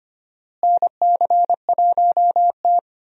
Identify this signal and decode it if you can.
Morse code, heard as a keyed tone, about 25 wpm, 700 Hz. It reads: NC1T
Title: USA Call Signs - 4 Characters - Repeated 3x - 25wpm